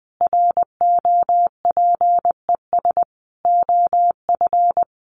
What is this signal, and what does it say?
Morse code, heard as a keyed tone, about 20 wpm, 700 Hz. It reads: ROPES OF
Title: A Sound Of Thunder 20wpm